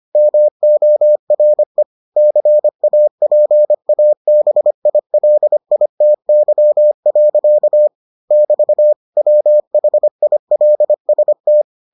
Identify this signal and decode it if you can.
Morse code, heard as a keyed tone, about 25 wpm, 600 Hz. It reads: MORE CAPABILITY. <BT> WHILST